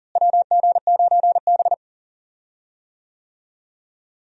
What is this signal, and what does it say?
Morse code, heard as a keyed tone, about 40 wpm, 700 Hz. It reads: WG9B
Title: USA Call Signs - 4 Characters - Repeated 2x - 40wpm